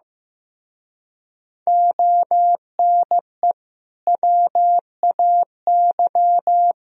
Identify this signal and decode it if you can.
Morse code, heard as a keyed tone, about 15 wpm, 700 Hz. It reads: ONE WAY
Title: Speed-Racing - Sets of 5 Words 15wpm